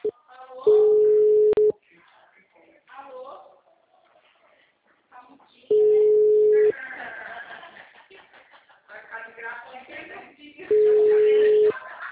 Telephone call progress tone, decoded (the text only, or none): ringback tone